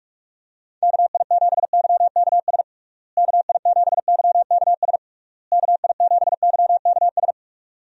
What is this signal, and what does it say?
Morse code, heard as a keyed tone, about 46 wpm, 700 Hz. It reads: KI7YKS KI7YKS KI7YKS
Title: USA Call Signs - 6 Characters - Repeated 3x - 45wpm